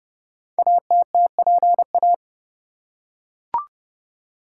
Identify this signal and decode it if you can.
Morse code, heard as a keyed tone, about 30 wpm, 700 Hz. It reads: ATTPA
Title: Code Groups - 5 Letters - 30wpm